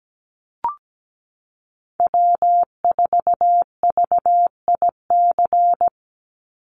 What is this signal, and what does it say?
Morse code, heard as a keyed tone, about 17 wpm, 700 Hz. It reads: W4VIC
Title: Call Signs Repeated 3x - 17wpm